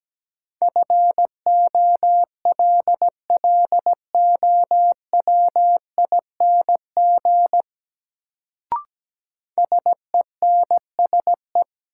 Morse code, heard as a keyed tone, about 17 wpm, 700 Hz. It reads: FOLLOWING SENSE
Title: Sentences from Top 700 Words - New Word Drill - 17wpm